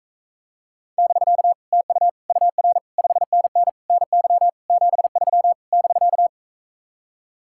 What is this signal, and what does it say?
Morse code, heard as a keyed tone, about 45 wpm, 700 Hz. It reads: <BK> TU UR 5NN NY 73 <BK>